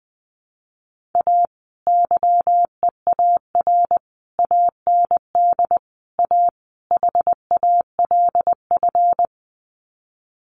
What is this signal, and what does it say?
Morse code, heard as a keyed tone, about 20 wpm, 700 Hz. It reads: A YEAR AND A HALF
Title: Sets of 5 Words 20wpm